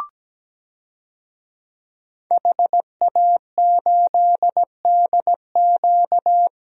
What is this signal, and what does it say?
Morse code, heard as a keyed tone, about 17 wpm, 700 Hz. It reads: HA8DQ
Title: DXCC-100 Call Signs - 5 Characters - Repeated 3x - 17wpm